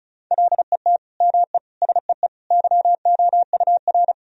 Morse code, heard as a keyed tone, about 36 wpm, 700 Hz. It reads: LET ME SEE YOUR
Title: Sentences from Top 1000 Words - Review Part 2 - 35wpm